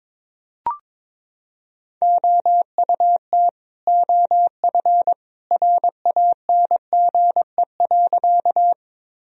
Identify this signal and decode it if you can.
Morse code, heard as a keyed tone, about 22 wpm, 700 Hz. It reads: OUT OF RANGE.
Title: Sentences from Top 1000 Words - Review Part 4 - 22wpm